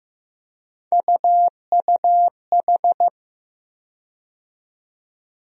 Morse code, heard as a keyed tone, about 15 wpm, 700 Hz. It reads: UUH
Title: USA Call Sign Postfixes - 3 Letters 15wpm